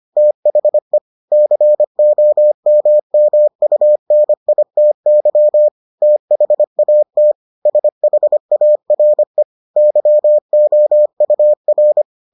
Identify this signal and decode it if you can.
Morse code, heard as a keyed tone, about 25 wpm, 600 Hz. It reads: THE COMMUNITY THAT SHARE YOUR